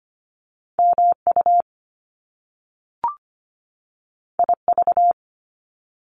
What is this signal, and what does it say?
Morse code, heard as a keyed tone, about 25 wpm, 700 Hz. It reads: MU IV
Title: Top Two Letter Combinations in Words 25wpm